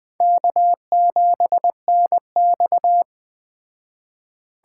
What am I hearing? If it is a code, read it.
Morse code, heard as a keyed tone, about 22 wpm, 700 Hz. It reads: K7NX